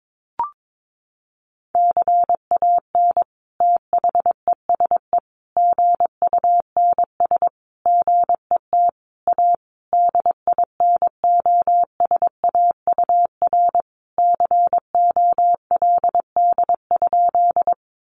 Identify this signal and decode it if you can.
Morse code, heard as a keyed tone, about 22 wpm, 700 Hz. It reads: CAN THESE GUNS GET A DINOSAUR COLD?
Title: A Sound Of Thunder 22wpm